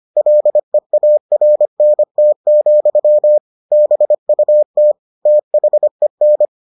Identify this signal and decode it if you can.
Morse code, heard as a keyed tone, about 25 wpm, 600 Hz. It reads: LEARNT, BUT THEN